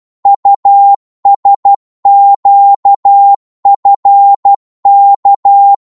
Morse code, heard as a keyed tone, about 12 wpm, 800 Hz. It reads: USQFK
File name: Morse / ltr-12wpm